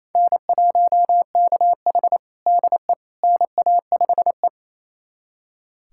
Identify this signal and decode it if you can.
Morse code, heard as a keyed tone, about 28 wpm, 700 Hz. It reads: N1KH DE NA5E
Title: USA Callsign DE Callsign - 4 Characters - 28wpm